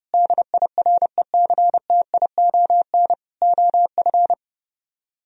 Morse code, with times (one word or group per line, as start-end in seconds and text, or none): 0.1-3.1 DIRECTION
3.4-4.3 OF